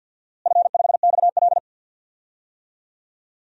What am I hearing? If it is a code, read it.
Morse code, heard as a keyed tone, about 46 wpm, 700 Hz. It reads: F5XL
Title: DXCC-100 Call Signs - 4 Characters - Repeated 3x - 50wpm